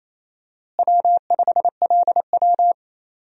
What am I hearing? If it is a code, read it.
Morse code, heard as a keyed tone, about 28 wpm, 700 Hz. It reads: W5LW
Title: USA Call Signs - 4 Characters - Repeated 3x - 28wpm